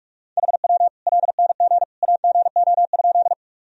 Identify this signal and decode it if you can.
Morse code, heard as a keyed tone, about 46 wpm, 700 Hz. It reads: HW LNG AGO?
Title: Sentences from Top 500 Words and Abbreviations 45wpm